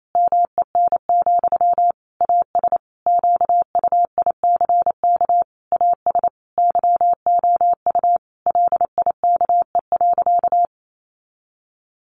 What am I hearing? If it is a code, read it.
Morse code, heard as a keyed tone, about 28 wpm, 700 Hz. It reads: MEN, AS QUICK AS YOU LIKE.
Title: Sentences from Top 500 Words 28wpm